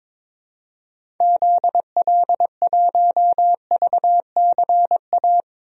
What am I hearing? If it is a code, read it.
Morse code, heard as a keyed tone, about 21 wpm, 700 Hz. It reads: ZL1VCA
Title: DXCC-100 Call Signs - 6 Characters - 22wpm